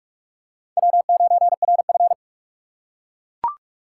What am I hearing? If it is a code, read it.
Morse code, heard as a keyed tone, about 42 wpm, 700 Hz. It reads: W9RF
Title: USA Call Signs - 4 Characters - 3x Character Spacing - 45wpm